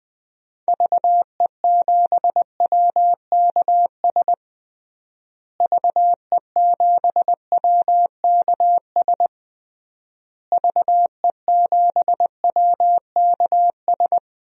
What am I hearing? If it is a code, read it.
Morse code, heard as a keyed tone, about 20 wpm, 700 Hz. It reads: VE7WKS VE7WKS VE7WKS
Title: DXCC-100 Call Signs - 6 Characters - Repeated 3x - 20wpm